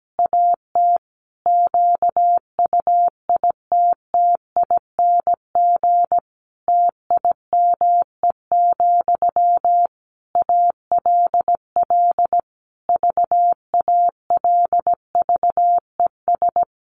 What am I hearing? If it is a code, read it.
Morse code, heard as a keyed tone, about 17 wpm, 700 Hz. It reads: AT QUITTING TIME, ALL VALVES
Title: A Sound Of Thunder 17wpm